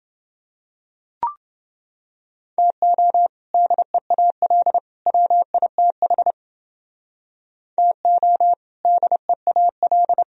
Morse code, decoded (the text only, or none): TO DEAL WITH TO DEAL